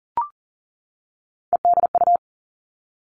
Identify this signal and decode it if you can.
Morse code, heard as a keyed tone, about 38 wpm, 700 Hz. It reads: EDU
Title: Three Challenging Letter-Number Combinations 40wpm